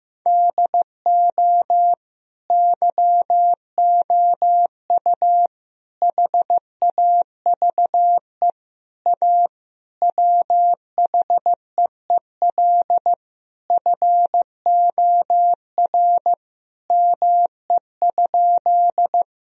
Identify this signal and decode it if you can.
Morse code, heard as a keyed tone, about 15 wpm, 700 Hz. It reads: DO YOU HAVE A WHEEL FOR ME?
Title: Sentences from Top 1000 Words - Review Part 3 - 15wpm